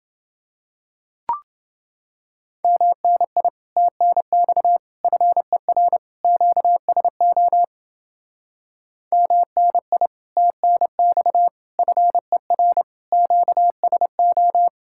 Morse code, heard as a keyed tone, about 30 wpm, 700 Hz. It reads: MNI TNX FER QSO MNI TNX FER QSO